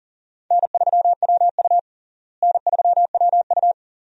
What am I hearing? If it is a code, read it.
Morse code, heard as a keyed tone, about 40 wpm, 700 Hz. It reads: N3WU N3WU
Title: USA Call Signs - 4 Characters - Repeated 2x - 40wpm